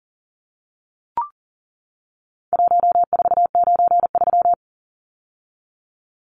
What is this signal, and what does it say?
Morse code, heard as a keyed tone, about 40 wpm, 700 Hz. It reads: 1493
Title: Numbers - Contest Style - 4-Digit 40wpm